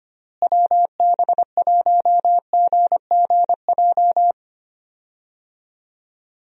Morse code, heard as a keyed tone, about 25 wpm, 700 Hz. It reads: WB1GGJ
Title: USA Call Signs - 6 Characters - Repeated 2x - 25wpm